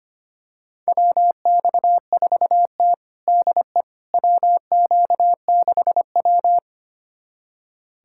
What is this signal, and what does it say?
Morse code, heard as a keyed tone, about 25 wpm, 700 Hz. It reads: WX4T DE WQ6W
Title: USA Callsign DE Callsign - 4 Characters - 25wpm